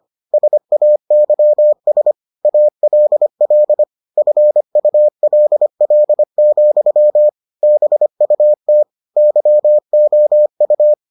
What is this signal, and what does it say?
Morse code, heard as a keyed tone, about 25 wpm, 600 Hz. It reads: SAYS ALL FULL, BUT YOU